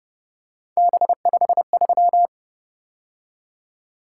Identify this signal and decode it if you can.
Morse code, heard as a keyed tone, about 30 wpm, 700 Hz. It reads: B53